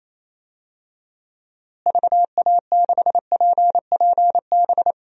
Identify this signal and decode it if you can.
Morse code, heard as a keyed tone, about 28 wpm, 700 Hz. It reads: VA6PPB
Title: DXCC-100 Call Signs - 6 Characters - Repeated 2x - 28wpm